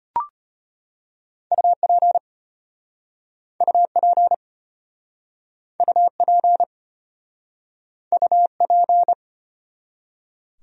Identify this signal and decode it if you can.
Morse code, heard as a keyed tone, about 34 wpm, 700 Hz. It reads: UP UP UP UP